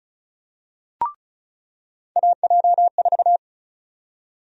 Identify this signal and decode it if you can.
Morse code, heard as a keyed tone, about 36 wpm, 700 Hz. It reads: AJ4
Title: USA Call Sign Prefixes - 35wpm